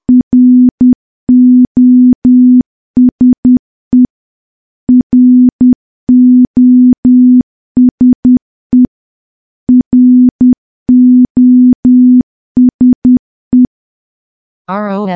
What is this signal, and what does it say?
Morse code, heard as a keyed tone, about 10 wpm, 250 Hz. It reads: ROSE ROSE ROSE
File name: morse / morse-training-10WPM